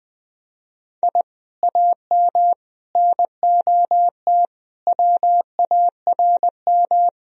Morse code, heard as a keyed tone, about 20 wpm, 700 Hz. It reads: I AM NOT WARM
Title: Sentences from Top 1000 Words - Review Part 2 - 20wpm